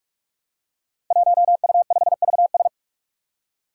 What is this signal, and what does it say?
Morse code, heard as a keyed tone, about 44 wpm, 700 Hz. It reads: 1U5VS
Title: Five Challenging Letter-Number Combinations 45wpm